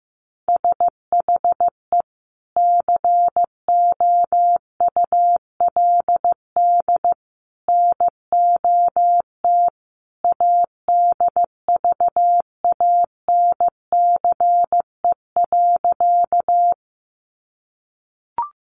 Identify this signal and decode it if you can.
Morse code, heard as a keyed tone, about 15 wpm, 700 Hz. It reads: SHE COULD NOT ADVANCE.
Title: Sentences from Top 1400 Words 15wpm